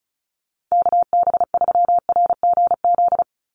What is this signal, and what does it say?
Morse code, heard as a keyed tone, about 34 wpm, 700 Hz. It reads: KB3RGZ
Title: USA Call Signs 35wpm